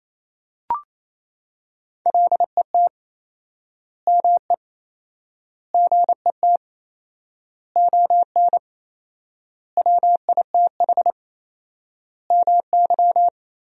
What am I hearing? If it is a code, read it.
Morse code, heard as a keyed tone, about 28 wpm, 700 Hz. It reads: LET ME GET ON WITH MY
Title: 4x Word-Spacing - Sentences from Top 300 Words - 28wpm